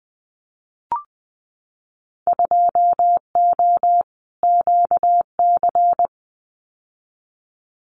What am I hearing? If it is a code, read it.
Morse code, heard as a keyed tone, about 20 wpm, 700 Hz. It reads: 2O QC
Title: Winter Field Day Exchanges 20wpm